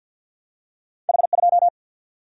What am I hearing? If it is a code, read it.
Morse code, heard as a keyed tone, about 52 wpm, 700 Hz. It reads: H 2
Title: Single-Letter Single-Number 50wpm